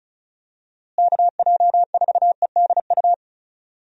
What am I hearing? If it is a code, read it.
Morse code, heard as a keyed tone, about 36 wpm, 700 Hz. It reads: KJ4EDU
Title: USA Call Signs - 6 Characters - Repeated 3x - 35wpm